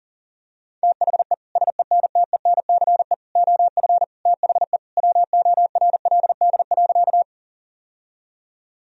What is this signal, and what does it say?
Morse code, heard as a keyed tone, about 40 wpm, 700 Hz. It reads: THE SENTENCE OF THE WORLD.
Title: Sentences from Top 1000 Words - Review Part 1 - 40wpm